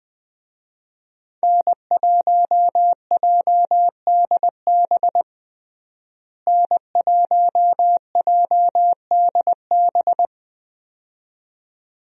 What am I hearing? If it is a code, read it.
Morse code, heard as a keyed tone, about 20 wpm, 700 Hz. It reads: N1JDB N1JDB
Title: USA Call Signs - 5 Characters - Repeated 2x - 20wpm